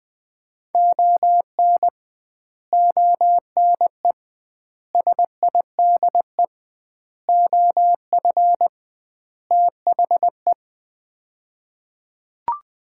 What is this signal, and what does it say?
Morse code, heard as a keyed tone, about 20 wpm, 700 Hz. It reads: ON ONE SIDE OF THE
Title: 2x Word-Spacing - Sets of 5 Words - 20wpm